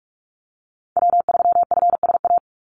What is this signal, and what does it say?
Morse code, heard as a keyed tone, about 44 wpm, 700 Hz. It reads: W3FSA
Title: Call Signs - 45wpm